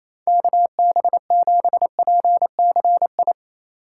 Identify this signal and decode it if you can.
Morse code, heard as a keyed tone, about 34 wpm, 700 Hz. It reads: KB7PCI
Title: Speed Racing - USA Call Signs 28wpm